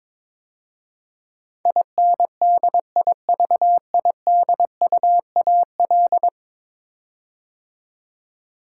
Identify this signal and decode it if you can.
Morse code, heard as a keyed tone, about 22 wpm, 700 Hz. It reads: INDIVIDUAL
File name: practice / 22wpm / 10 Letter Words - First 3 Letters Given - 22wpm